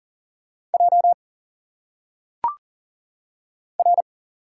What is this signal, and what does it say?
Morse code, heard as a keyed tone, about 40 wpm, 700 Hz. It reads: J R
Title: Single Letters - Phonetic Pronunciation 40wpm